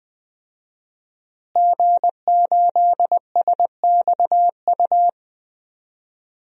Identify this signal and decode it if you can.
Morse code, heard as a keyed tone, about 21 wpm, 700 Hz. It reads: G8SXU
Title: DXCC-100 Call Signs - 5 Characters - Repeated 2x - 20wpm